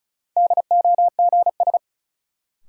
Morse code, heard as a keyed tone, about 36 wpm, 700 Hz. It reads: DOGS
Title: Sentences from Top 1600 Words - New Word Drill - 35wpm